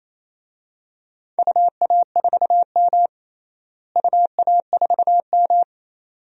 Morse code, heard as a keyed tone, about 28 wpm, 700 Hz. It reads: UA4M UA4M
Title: DXCC-100 Call Signs - 4 Characters - Repeated 2x - 28wpm